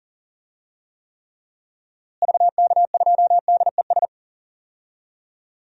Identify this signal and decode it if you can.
Morse code, heard as a keyed tone, about 40 wpm, 700 Hz. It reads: VK2DES
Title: DXCC-100 Call Signs - 6 Characters - Repeated 2x - 40wpm